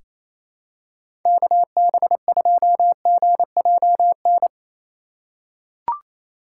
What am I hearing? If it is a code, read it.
Morse code, heard as a keyed tone, about 27 wpm, 700 Hz. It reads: KB2GJN
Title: USA Call Signs - 6 Characters - 3x Character Spacing - 28wpm